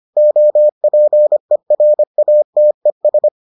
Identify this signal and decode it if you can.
Morse code, heard as a keyed tone, about 25 wpm, 600 Hz. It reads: OPERATES